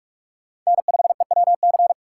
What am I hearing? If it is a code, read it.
Morse code, heard as a keyed tone, about 44 wpm, 700 Hz. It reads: N5EWC